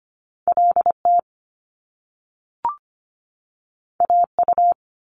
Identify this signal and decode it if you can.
Morse code, heard as a keyed tone, about 25 wpm, 700 Hz. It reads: LT AU
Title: Top Two Letter Combinations in Words 25wpm